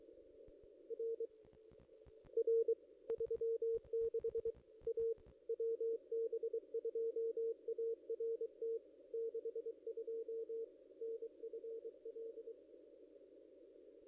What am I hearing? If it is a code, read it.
Morse code, heard as a keyed tone, about 23 wpm, 460 Hz. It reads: R R 36 A WB2ART 62 NFL
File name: contests / ss-cw-k5tr-op-n5rz -1700